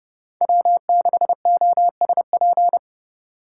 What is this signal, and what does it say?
Morse code, heard as a keyed tone, about 30 wpm, 700 Hz. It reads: W6OSP
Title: K1USN SST QSO Practice - 30wpm